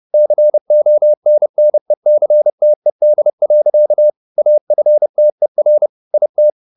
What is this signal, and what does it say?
Morse code, heard as a keyed tone, about 30 wpm, 600 Hz. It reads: CONNECTED. AFTER IT
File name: the old man and the sea / the old man and the sea 1 30wpm